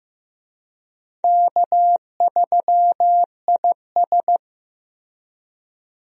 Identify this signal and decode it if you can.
Morse code, heard as a keyed tone, about 15 wpm, 700 Hz. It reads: K3IS
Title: USA Call Signs - 4 Characters - Repeated 2x - 15wpm